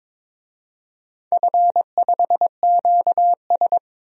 Morse code, heard as a keyed tone, about 22 wpm, 690 Hz. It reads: F5QS
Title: DXCC-100 Call Signs - 4 Characters - 22wpm